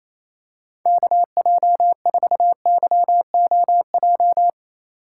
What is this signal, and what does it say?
Morse code, heard as a keyed tone, about 28 wpm, 700 Hz. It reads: KJ4YOJ
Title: USA Call Signs - 6 Characters - 2x Character Spacing - 28wpm